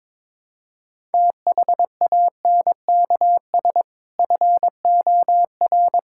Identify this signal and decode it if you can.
Morse code, heard as a keyed tone, about 22 wpm, 700 Hz. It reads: THANKS FOR
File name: practice / 22wpm / Sentences with 2-Word Phrases - 22wpm